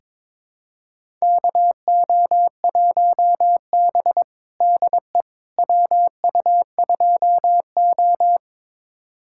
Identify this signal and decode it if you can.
Morse code, heard as a keyed tone, about 22 wpm, 700 Hz. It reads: KO1B DE WU2O
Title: USA Callsign DE Callsign - 4 Characters - 22wpm